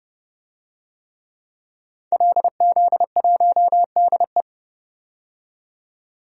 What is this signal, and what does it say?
Morse code, heard as a keyed tone, about 30 wpm, 700 Hz. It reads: LZ1DE
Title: DXCC-100 Call Signs - 5 Characters - 30wpm